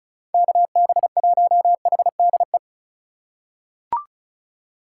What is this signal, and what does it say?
Morse code, heard as a keyed tone, about 36 wpm, 700 Hz. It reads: KB1HDE
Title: USA Call Signs - 6 Characters - 3x Character Spacing - 35wpm